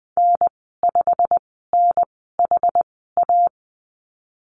Morse code, heard as a keyed tone, about 20 wpm, spaced as for 15 wpm, 700 Hz. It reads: N5NHA
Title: 128 Lesson 21g - Number 6 - Call Signs 20wpm